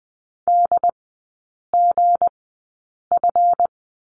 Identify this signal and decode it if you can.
Morse code, heard as a keyed tone, about 20 wpm, spaced as for 8 wpm, 690 Hz. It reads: DGF
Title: 159 Lesson 26e - Letter g - Three Characters 20wpm